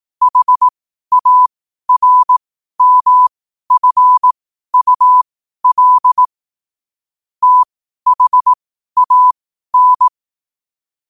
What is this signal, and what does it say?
Morse code, heard as a keyed tone, about 18 wpm, spaced as for 12 wpm, 1000 Hz. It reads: HARMFUL THAN
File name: qotdfiles / 12wpm